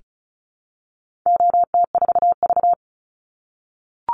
Morse code, heard as a keyed tone, about 34 wpm, 700 Hz. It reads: OT4V
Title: Call Signs - 35wpm